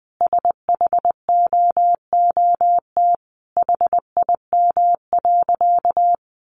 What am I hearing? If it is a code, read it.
Morse code, heard as a keyed tone, about 20 wpm, 700 Hz. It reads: SHOOT HIM.
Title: Sentences from Top 1200 Words 20wpm